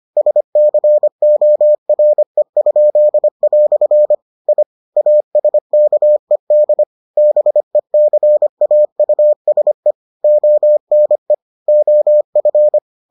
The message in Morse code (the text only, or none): SCORE?" I ASKED BECAUSE ONE OF